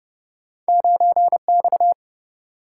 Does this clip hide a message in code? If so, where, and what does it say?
Yes: Morse code from 0.7 to 1.9 s, heard as a keyed tone, about 31 wpm, 700 Hz. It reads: 9X